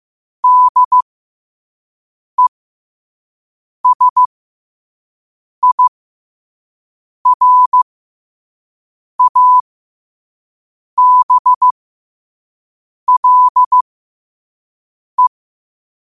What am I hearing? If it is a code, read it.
Morse code, heard as a keyed tone, about 15 wpm, spaced as for 5 wpm, 1000 Hz. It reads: DESIRABLE
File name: qotdfiles / 5wpm